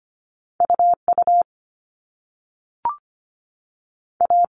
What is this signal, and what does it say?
Morse code, heard as a keyed tone, about 25 wpm, 700 Hz. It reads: UU A
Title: All Cut Numbers - 2-Digit 25wpm